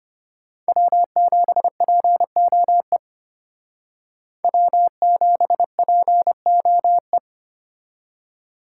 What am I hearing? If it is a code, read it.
Morse code, heard as a keyed tone, about 33 wpm, 700 Hz. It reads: W7POE W7POE